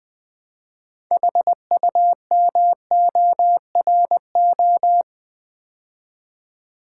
Morse code, heard as a keyed tone, about 20 wpm, 700 Hz. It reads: HUMORO
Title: Build A Word - 2 Letters At A Time - 8-Letter Words - 20wpm